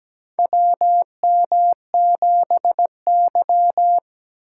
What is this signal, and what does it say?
Morse code, heard as a keyed tone, about 17 wpm, 700 Hz. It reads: WM7Y
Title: USA Call Signs - 4x Character Spacing - 17wpm